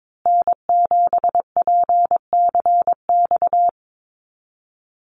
Morse code, heard as a keyed tone, about 22 wpm, 700 Hz. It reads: N7PCX
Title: USA Call Signs Repeated 3x 22wpm